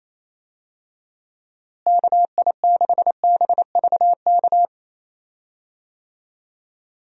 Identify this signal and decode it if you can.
Morse code, heard as a keyed tone, about 28 wpm, 700 Hz. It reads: KI6BVK
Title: USA Call Signs - 6 Characters - Repeated 3x - 28wpm